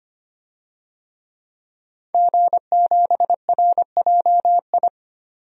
Morse code, heard as a keyed tone, about 25 wpm, 700 Hz. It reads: G7RJI
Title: DXCC-100 Call Signs - 5 Characters - 25wpm